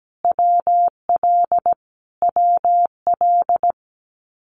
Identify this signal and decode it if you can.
Morse code, heard as a keyed tone, about 17 wpm, 700 Hz. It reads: WL WL
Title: Abbreviations Repeated 3x - 17wpm